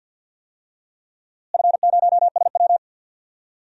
Morse code, heard as a keyed tone, about 51 wpm, 700 Hz. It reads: F0SW
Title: DXCC-100 Call Signs - 4 Characters - Repeated 3x - 50wpm